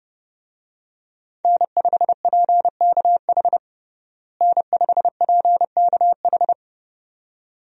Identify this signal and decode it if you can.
Morse code, heard as a keyed tone, about 30 wpm, 700 Hz. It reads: N5PKH N5PKH
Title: USA Call Signs - 5 Characters - Repeated 2x - 30wpm